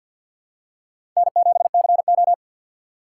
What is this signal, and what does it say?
Morse code, heard as a keyed tone, about 53 wpm, 700 Hz. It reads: N7CO
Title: USA Call Signs - 4 Characters - 6x Character Spacing - 50wpm